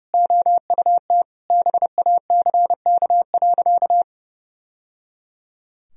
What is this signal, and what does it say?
Morse code, heard as a keyed tone, about 30 wpm, 700 Hz. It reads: OUT BACK.
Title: Sentences from Top 1000 Words - Review Part 2 - 30wpm